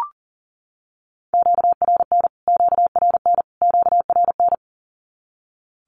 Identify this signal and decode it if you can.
Morse code, heard as a keyed tone, about 40 wpm, 700 Hz. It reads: QRN QRN QRN
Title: Q-Codes Repeated 3x 40wpm